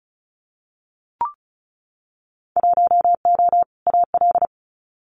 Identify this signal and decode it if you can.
Morse code, heard as a keyed tone, about 36 wpm, 700 Hz. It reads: 1O AL
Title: Winter Field Day Exchanges 35wpm